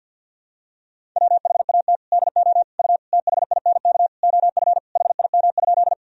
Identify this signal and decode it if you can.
Morse code, heard as a keyed tone, about 50 wpm, 700 Hz. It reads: WHAT DO U THINK OF HIM?
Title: Sentences from Top 200 Words and Abbreviations 50wpm